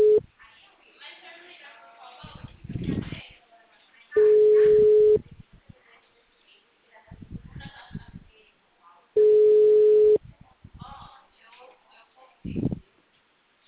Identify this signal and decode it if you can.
Ringback tone (Europe, 425 Hz), with speech.
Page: Ribeirao